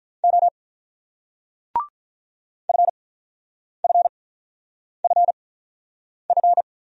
Morse code, heard as a keyed tone, about 49 wpm, spaced as for 9 wpm, 700 Hz. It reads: O FFFF